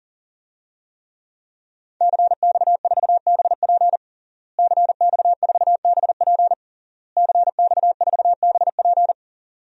Morse code, heard as a keyed tone, about 40 wpm, 700 Hz. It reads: CX4BP CX4BP CX4BP
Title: DXCC-100 Call Signs - 5 Characters - Repeated 3x - 40wpm